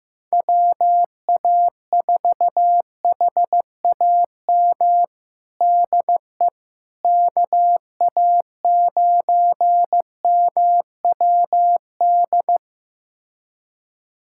Morse code, heard as a keyed tone, about 15 wpm, 700 Hz. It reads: WA4HAM DE KA9MWD
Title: USA Callsign DE Callsign - 6 Characters - 15wpm